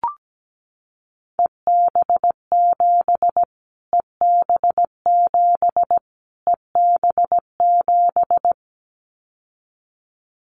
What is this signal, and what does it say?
Morse code, heard as a keyed tone, about 17 wpm, 700 Hz. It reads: EB7 EB7 EB7
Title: Callsign Prefixes for Top 25 DXCC - Repeated 3x - 17wpm